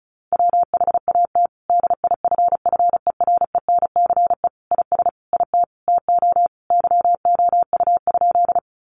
Morse code, heard as a keyed tone, about 34 wpm, 700 Hz. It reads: WHAT DIFFERENCE IS IT TO YOU?
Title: Sentences from Top 800 Words 35wpm